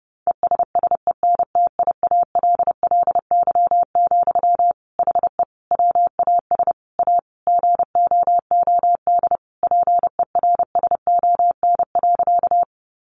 Morse code, heard as a keyed tone, about 30 wpm, 700 Hz. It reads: ESSENTIALLY, HE WAS A GOOD PERSON.
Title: Sentences from Top 1300 Words 30wpm